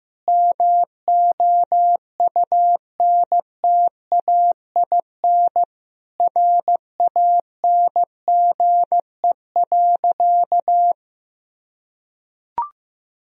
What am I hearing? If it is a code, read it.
Morse code, heard as a keyed tone, about 15 wpm, 700 Hz. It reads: MOUNTAIN RANGE.
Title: Sentences from Top 1000 Words - Review Part 4 - 15wpm